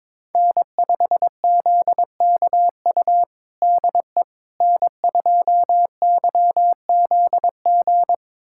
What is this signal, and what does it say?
Morse code, heard as a keyed tone, about 21 wpm, 700 Hz. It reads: N5ZKU DE N2YZG
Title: USA Callsign DE Callsign - 5 Characters - 22wpm